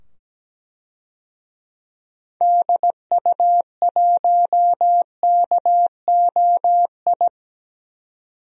Morse code, heard as a keyed tone, about 17 wpm, 700 Hz. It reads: DU1KOI
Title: DXCC-100 Call Signs - 6 Characters - 17wpm